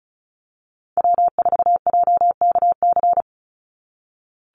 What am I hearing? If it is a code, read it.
Morse code, heard as a keyed tone, about 34 wpm, 700 Hz. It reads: W4JKC
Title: NA SOTA Chaser Callsigns 35wpm